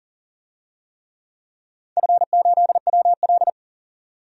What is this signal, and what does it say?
Morse code, heard as a keyed tone, about 37 wpm, 700 Hz. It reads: F8WL
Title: DXCC-100 Call Signs - 4 Characters - 40wpm